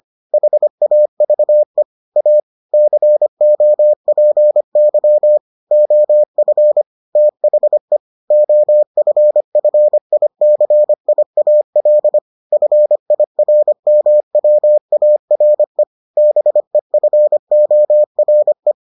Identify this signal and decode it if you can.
Morse code, heard as a keyed tone, about 25 wpm, 600 Hz. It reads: HAVE A COPY OF THE OFFICIAL FIRMWARE BEFORE